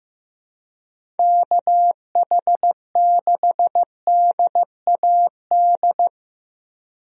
Morse code, heard as a keyed tone, about 15 wpm, 700 Hz. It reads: KH6DAD
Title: DXCC-100 Call Signs - 6 Characters - 15wpm